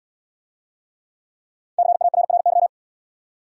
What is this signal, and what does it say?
Morse code, heard as a keyed tone, about 68 wpm, 700 Hz. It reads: CARRY